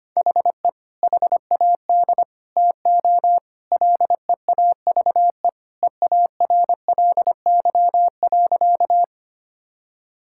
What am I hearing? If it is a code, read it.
Morse code, heard as a keyed tone, about 25 wpm, 700 Hz. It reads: HE HAD TO LEAVE EARLY.
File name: practice / 25wpm / Sentences with 2-Word Phrases - 25wpm